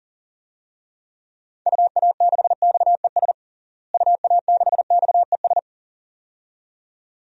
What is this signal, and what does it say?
Morse code, heard as a keyed tone, about 40 wpm, 700 Hz. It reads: UA6XES UA6XES
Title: DXCC-100 Call Signs - 6 Characters - Repeated 2x - 40wpm